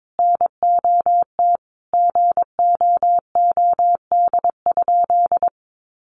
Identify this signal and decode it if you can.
Morse code, heard as a keyed tone, about 22 wpm, 700 Hz. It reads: NOT GOOD?
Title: Sentences from Top 1400 Words 22wpm